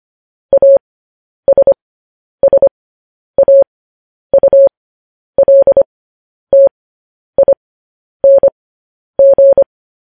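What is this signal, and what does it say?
Morse code, heard as a keyed tone, about 25 wpm, spaced as for 10 wpm, 550 Hz. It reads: ASSAULTING
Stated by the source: Morse code news headlines broadcast, opening with vvv.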